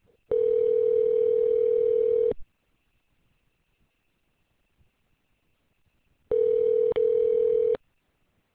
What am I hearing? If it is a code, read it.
Ringback tone (North America), 440 + 480 Hz.